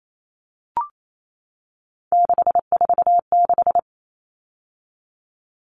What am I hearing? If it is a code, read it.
Morse code, heard as a keyed tone, about 28 wpm, 700 Hz. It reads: 646
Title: Numbers - 3-Digit 28wpm